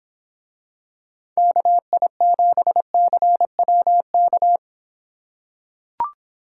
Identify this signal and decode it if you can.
Morse code, heard as a keyed tone, about 26 wpm, 700 Hz. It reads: KI7CWK